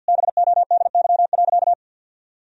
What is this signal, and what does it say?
Morse code, heard as a keyed tone, about 50 wpm, 700 Hz. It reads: BODY.
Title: Sentences from Top 1000 Words - Review Part 4 - 50wpm